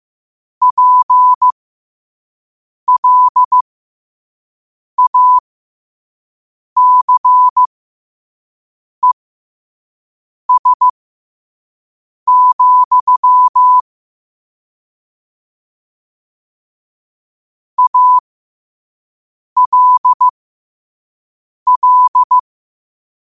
Morse code, heard as a keyed tone, about 15 wpm, spaced as for 5 wpm, 1000 Hz. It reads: PLACES, ALL
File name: qotdfiles / 5wpm